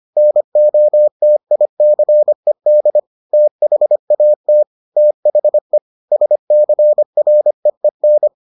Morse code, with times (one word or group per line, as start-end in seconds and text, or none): 0.2-3.0 NOTICED
3.3-4.6 THAT
5.0-5.8 THE
6.1-8.3 SCREEN